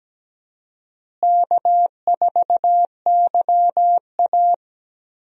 Morse code, heard as a keyed tone, about 17 wpm, 700 Hz. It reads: K4YA
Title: USA Call Signs - 4 Characters - Repeated 2x - 17wpm